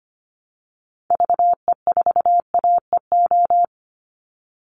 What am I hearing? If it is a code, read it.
Morse code, heard as a keyed tone, about 25 wpm, 700 Hz. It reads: VE4AEO
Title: Call Signs Repeated 3x - 25wpm